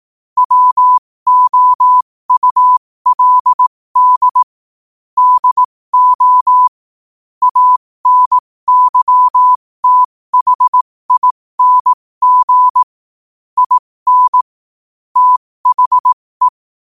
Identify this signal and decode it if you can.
Morse code, heard as a keyed tone, about 18 wpm, spaced as for 15 wpm, 1000 Hz. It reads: WOULD DO ANYTHING IN THE
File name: qotdfiles / 15wpm